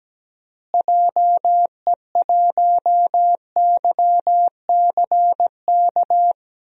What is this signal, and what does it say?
Morse code, heard as a keyed tone, about 17 wpm, 700 Hz. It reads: JE1YCK
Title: DXCC-100 Call Signs - 6 Characters - Repeated 3x - 17wpm